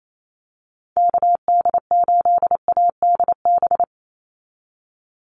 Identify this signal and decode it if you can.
Morse code, heard as a keyed tone, about 28 wpm, 700 Hz. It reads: KD8ADB
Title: USA Call Signs 28wpm